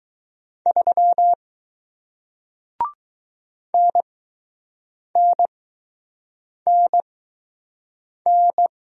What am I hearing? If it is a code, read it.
Morse code, heard as a keyed tone, about 20 wpm, spaced as for 6 wpm, 700 Hz. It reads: 3 NNNN